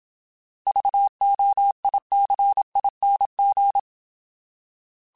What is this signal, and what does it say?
Morse code, heard as a keyed tone, about 27 wpm, 800 Hz. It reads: VOICING